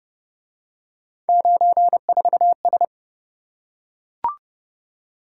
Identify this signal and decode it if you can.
Morse code, heard as a keyed tone, about 29 wpm, 700 Hz. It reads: 94S